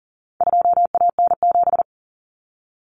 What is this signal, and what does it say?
Morse code, heard as a keyed tone, about 40 wpm, 700 Hz. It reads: 2AN7
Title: Common Cut Numbers - 4-Digit 40wpm